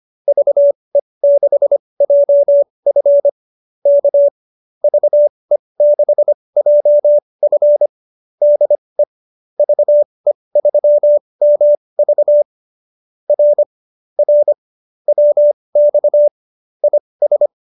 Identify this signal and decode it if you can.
Morse code, heard as a keyed tone, about 25 wpm, spaced as for 20 wpm, 590 Hz. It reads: VE6JF K VE6JF DE VE3MV R R WX IS
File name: QSO303 20